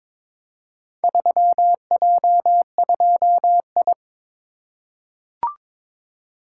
Morse code, heard as a keyed tone, about 23 wpm, 700 Hz. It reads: 3J2I
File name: practice / 22wpm / Four Letter-Number Combinations - 22wpm